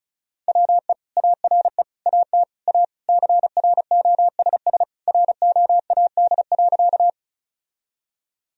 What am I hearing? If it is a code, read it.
Morse code, heard as a keyed tone, about 34 wpm, 700 Hz. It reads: WE ARE AT A CROSS ROAD.
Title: Sentences from Top 1000 Words - Review Part 2 - 35wpm